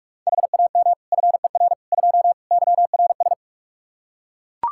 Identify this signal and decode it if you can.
Morse code, heard as a keyed tone, about 41 wpm, 700 Hz. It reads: HAM FER 2 YRS
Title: Practice QSO - Exchange 3 - More Info - 45wpm